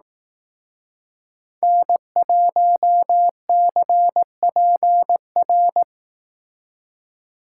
Morse code, heard as a keyed tone, about 19 wpm, 700 Hz. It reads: N1CPR
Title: Speed Racing - USA 1x3 Call Signs 15wpm